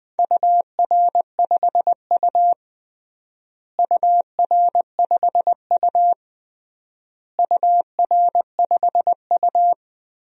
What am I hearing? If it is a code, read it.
Morse code, heard as a keyed tone, about 20 wpm, 700 Hz. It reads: UR5U UR5U UR5U
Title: DXCC-100 Call Signs - 4 Characters - Repeated 3x - 20wpm